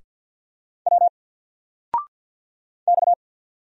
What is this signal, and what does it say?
Morse code, heard as a keyed tone, about 48 wpm, 700 Hz. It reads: W X
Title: Single Character (Letter:Number:Punctuation:Prosign) 50wpm